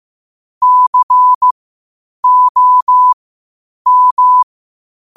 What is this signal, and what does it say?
Morse code, heard as a keyed tone, about 15 wpm, spaced as for 8 wpm, 1000 Hz. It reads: COM
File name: qotdfiles / 8wpm